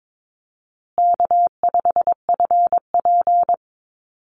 Morse code, heard as a keyed tone, about 22 wpm, 700 Hz. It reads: K5FP
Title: Easy Call Signs Repeated 3x - 22wpm